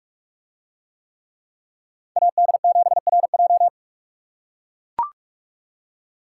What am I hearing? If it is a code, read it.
Morse code, heard as a keyed tone, about 44 wpm, 700 Hz. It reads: AD7RJ